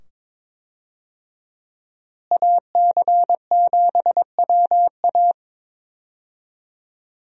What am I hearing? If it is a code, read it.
Morse code, heard as a keyed tone, about 22 wpm, 700 Hz. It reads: AC7WA
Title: USA Call Signs - 5 Characters - Repeated 2x - 22wpm